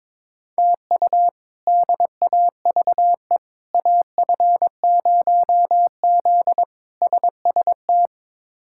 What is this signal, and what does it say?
Morse code, heard as a keyed tone, about 22 wpm, 700 Hz. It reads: TU DAVE AF0Z SST
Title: K1USN SST QSO Practice - 22wpm